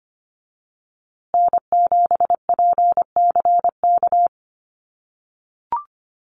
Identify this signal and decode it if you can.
Morse code, heard as a keyed tone, about 25 wpm, 700 Hz. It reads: N7PCK
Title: USA Call Signs Repeated 2x 25wpm